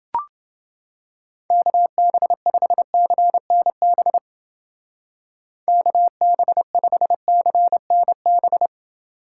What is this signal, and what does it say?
Morse code, heard as a keyed tone, about 28 wpm, 700 Hz. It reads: KB5CNB KB5CNB